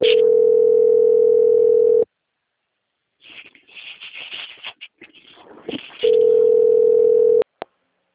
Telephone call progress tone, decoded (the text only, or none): ringback tone